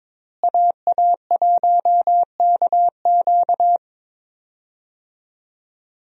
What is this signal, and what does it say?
Morse code, heard as a keyed tone, about 21 wpm, 700 Hz. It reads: AA1KQ
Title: USA Call Signs - 5 Characters - Repeated 3x - 22wpm